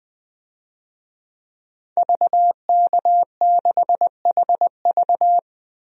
Morse code, heard as a keyed tone, about 20 wpm, 700 Hz. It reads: VK6HV
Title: DXCC-100 Call Signs - 5 Characters - 20wpm